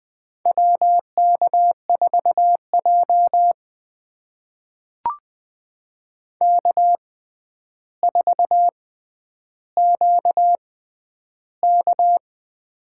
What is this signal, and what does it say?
Morse code, heard as a keyed tone, about 20 wpm, spaced as for 10 wpm, 700 Hz. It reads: WK4J K4QK
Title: USA Call Signs - 4 Characters - 6x Character Spacing - 20wpm